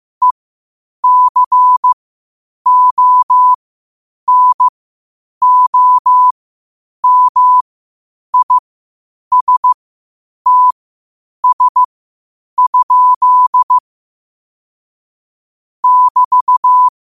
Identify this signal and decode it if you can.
Morse code, heard as a keyed tone, about 15 wpm, spaced as for 8 wpm, 1000 Hz. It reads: ECONOMISTS? <BT>
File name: qotdfiles / 8wpm